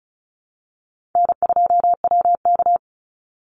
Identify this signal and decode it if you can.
Morse code, heard as a keyed tone, about 34 wpm, 700 Hz. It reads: N2WK
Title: Easy Call Signs Repeated 2x - 35wpm